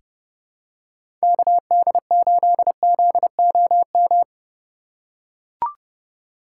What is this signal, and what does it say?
Morse code, heard as a keyed tone, about 32 wpm, 700 Hz. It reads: KD8ZOM
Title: USA Call Signs - 6 Characters - 3x Character Spacing - 30wpm